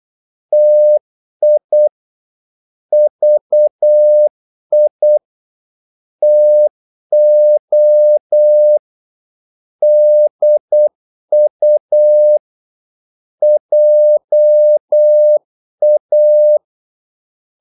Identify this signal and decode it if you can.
Morse code, heard as a keyed tone, about 8 wpm, 600 Hz. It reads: TI VI TO DU JA
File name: Korte ord DK 8wpm